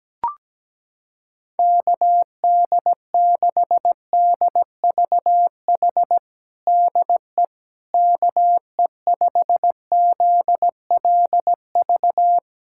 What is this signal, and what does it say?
Morse code, heard as a keyed tone, about 17 wpm, 700 Hz. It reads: KD6DVH DE KE5ZLV
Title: USA Callsign DE Callsign 17wpm